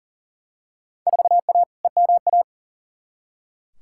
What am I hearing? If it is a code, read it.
Morse code, heard as a keyed tone, about 40 wpm, 700 Hz. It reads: 4A EMA
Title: Field Day Exchanges - 40wpm